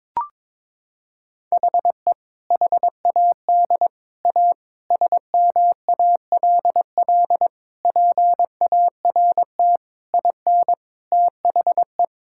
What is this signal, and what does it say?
Morse code, heard as a keyed tone, about 22 wpm, 700 Hz. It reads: HE HAD A SMALL PART IN THE
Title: Sentences from Top 1000 Words - Review Part 5 - 22wpm